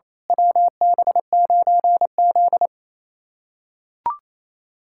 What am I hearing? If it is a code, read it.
Morse code, heard as a keyed tone, about 27 wpm, 690 Hz. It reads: WB9Z
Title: USA Callsign DE Callsign - 4 Characters - 28wpm